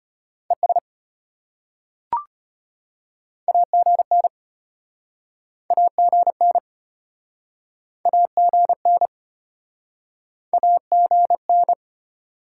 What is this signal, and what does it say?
Morse code, heard as a keyed tone, about 36 wpm, 700 Hz. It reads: ES AGN AGN AGN AGN